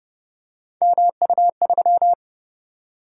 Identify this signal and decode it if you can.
Morse code, heard as a keyed tone, about 30 wpm, 700 Hz. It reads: MU3